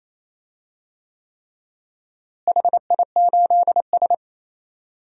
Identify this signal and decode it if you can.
Morse code, heard as a keyed tone, about 28 wpm, 700 Hz. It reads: HI8S
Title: DXCC-100 Call Signs - 4 Characters - Repeated 3x - 28wpm